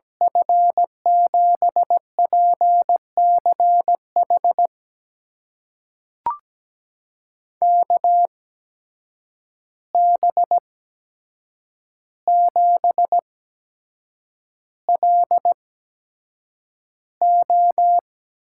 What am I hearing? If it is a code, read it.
Morse code, heard as a keyed tone, about 17 wpm, spaced as for 7 wpm, 700 Hz. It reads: F7PCH KB7LO
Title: USA Call Signs - 6 Characters - 8x Character Spacing - 17wpm